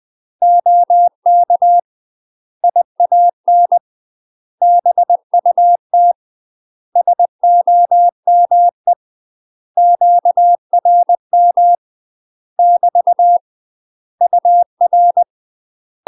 Morse code, heard as a keyed tone, about 20 wpm, 700 Hz. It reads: OK IAN BUT SOME QRM <BT> UR
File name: UK MASTER QSOs TRAINING / 20WPM / UK MASTER QSOs 0048